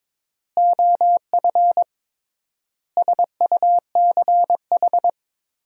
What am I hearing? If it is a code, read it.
Morse code, heard as a keyed tone, about 22 wpm, 700 Hz. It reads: OF SUCH
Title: 3x Word-Spacing - Sentences from Top 200 Words - 22wpm